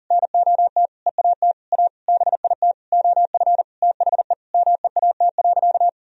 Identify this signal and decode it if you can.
Morse code, heard as a keyed tone, about 37 wpm, 700 Hz. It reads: NOT EAT A BIT OF THE MEAT.
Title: Sentences from Top 1000 Words - Review Part 3 - 40wpm